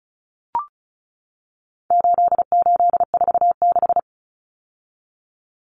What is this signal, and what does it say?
Morse code, heard as a keyed tone, about 34 wpm, 700 Hz. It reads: 8846
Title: Numbers - 4-Digit 35wpm